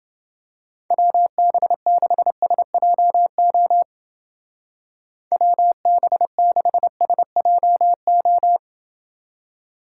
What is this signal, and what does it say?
Morse code, heard as a keyed tone, about 29 wpm, 700 Hz. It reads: WB6SJO WB6SJO